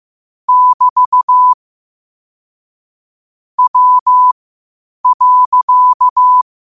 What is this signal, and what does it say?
Morse code, heard as a keyed tone, about 15 wpm, spaced as for 8 wpm, 1000 Hz. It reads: <BT> W.
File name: qotdfiles / 8wpm